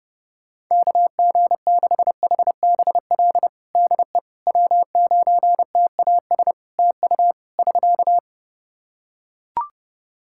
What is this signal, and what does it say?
Morse code, heard as a keyed tone, about 30 wpm, 700 Hz. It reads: KG6HBL DE W9TAS TU <SK>
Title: Practice QSO - Exchange 4 - Ending - 30wpm